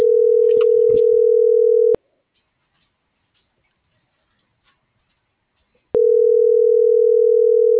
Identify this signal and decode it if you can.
Ringback tone (North America), 440 + 480 Hz.